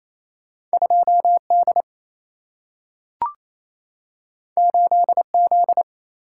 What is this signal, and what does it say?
Morse code, heard as a keyed tone, about 28 wpm, 700 Hz. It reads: 2D 8Z
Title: Single-Letter Single-Number 28wpm